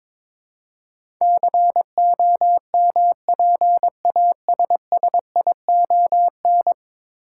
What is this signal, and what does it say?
Morse code, heard as a keyed tone, about 22 wpm, 700 Hz. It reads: COMPASSION
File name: practice / 22wpm / 10 Letter Words - First 3 Letters Given - 22wpm